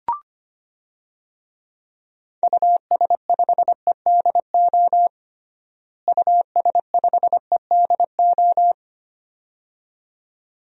Morse code, heard as a keyed tone, about 25 wpm, 700 Hz. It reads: US5EDO US5EDO
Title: DXCC-100 Call Signs - 6 Characters - Repeated 2x - 25wpm